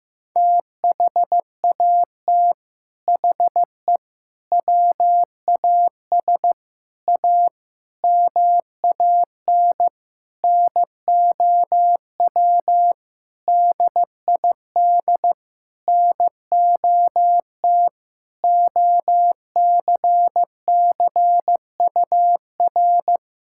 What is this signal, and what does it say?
Morse code, heard as a keyed tone, about 16 wpm, 700 Hz. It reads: THAT HE WAS A MAN NOW DID NOT OCCUR
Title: Sentences from Top 1000 Words - Review Part 6 - 15wpm